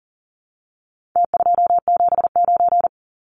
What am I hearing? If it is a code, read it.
Morse code, heard as a keyed tone, about 40 wpm, 700 Hz. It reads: T279
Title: Numbers - Contest Style - 4-Digit 40wpm